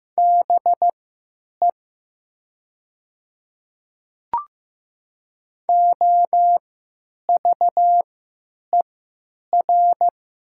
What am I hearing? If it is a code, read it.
Morse code, heard as a keyed tone, about 15 wpm, spaced as for 9 wpm, 690 Hz. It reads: BE OVER
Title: 3x Character Spacing - Top 100 Words - 15wpm